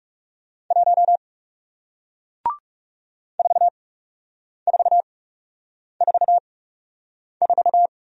Morse code, heard as a keyed tone, about 37 wpm, spaced as for 8 wpm, 700 Hz. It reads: 1 4444
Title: Speed-Racing Single Letter-Number 30wpm